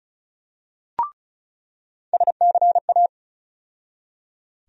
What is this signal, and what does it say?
Morse code, heard as a keyed tone, about 36 wpm, 700 Hz. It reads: SCA
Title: USA Call Sign Postfixes - 3 Letters 35wpm